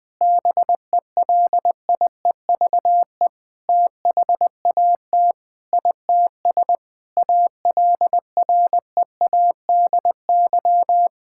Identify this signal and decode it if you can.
Morse code, heard as a keyed tone, about 20 wpm, 700 Hz. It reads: BELIEVE THAT ITS ALREADY
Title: Sentences with 7-Word Phrases - Spoken Beginning - 20wpm